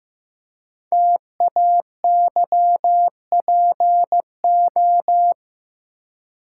Morse code, heard as a keyed tone, about 15 wpm, 700 Hz. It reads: TAYPO
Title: Code Groups - 5 Letters - 15wpm